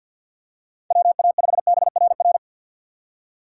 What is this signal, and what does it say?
Morse code, heard as a keyed tone, about 53 wpm, 700 Hz. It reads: WA5BRR
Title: USA Call Signs - 6 Characters - 4x Character Spacing - 50wpm